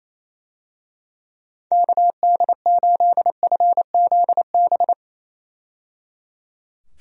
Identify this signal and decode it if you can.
Morse code, heard as a keyed tone, about 27 wpm, 700 Hz. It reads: KD8FZB
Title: USA Call Signs - 6 Characters - 28wpm